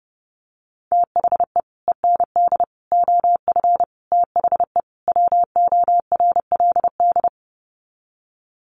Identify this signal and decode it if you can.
Morse code, heard as a keyed tone, about 30 wpm, 700 Hz. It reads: THE END OF THE WORLD
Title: Sets of 5 Words 30wpm